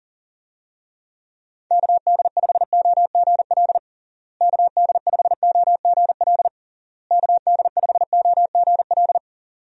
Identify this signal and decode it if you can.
Morse code, heard as a keyed tone, about 40 wpm, 700 Hz. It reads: KD5OGL KD5OGL KD5OGL
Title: USA Call Signs - 6 Characters - Repeated 3x - 40wpm